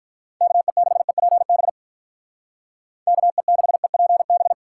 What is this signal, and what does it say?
Morse code, heard as a keyed tone, about 46 wpm, 700 Hz. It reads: KE6EPB KE6EPB